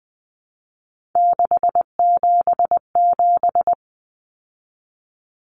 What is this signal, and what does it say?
Morse code, heard as a keyed tone, about 20 wpm, 700 Hz. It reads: 677
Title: Numbers - 3-Digit 20wpm